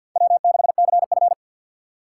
Morse code, heard as a keyed tone, about 52 wpm, 700 Hz. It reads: W6CF
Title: USA Call Signs - 4 Characters - 6x Character Spacing - 50wpm